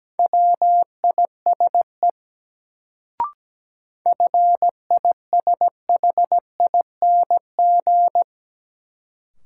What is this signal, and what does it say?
Morse code, heard as a keyed tone, about 17 wpm, 700 Hz. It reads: WISE FISHING
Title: Sentences from Top 1600 Words - New Word Drill Alt - 17wpm